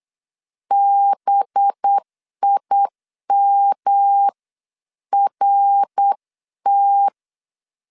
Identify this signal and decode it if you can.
Morse code, heard as a keyed tone, about 8 wpm, 800 Hz. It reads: BIMRT
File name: LEZIONI / 6 PKX0 / 6 PKX 8wpm 3041